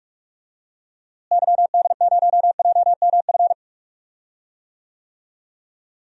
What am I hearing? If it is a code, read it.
Morse code, heard as a keyed tone, about 42 wpm, 700 Hz. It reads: YD0JMF
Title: DXCC-100 Call Signs - 6 Characters - Repeated 3x - 45wpm